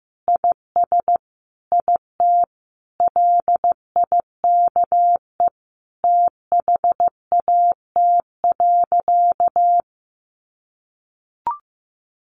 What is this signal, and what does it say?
Morse code, heard as a keyed tone, about 15 wpm, 700 Hz. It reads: IS IT LIKE THAT.
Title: Sentences from Top 500 Words 15wpm